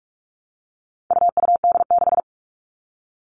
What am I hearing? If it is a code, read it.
Morse code, heard as a keyed tone, about 44 wpm, 700 Hz. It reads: UUD6
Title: All Cut Numbers - 4-Digit 45wpm